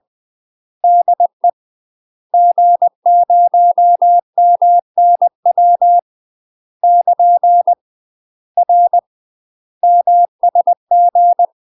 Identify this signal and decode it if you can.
Morse code, heard as a keyed tone, about 20 wpm, 700 Hz. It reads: DE G0MNW <KN> R MSG
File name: UK MASTER QSOs TRAINING / 20WPM / UK MASTER QSOs 0023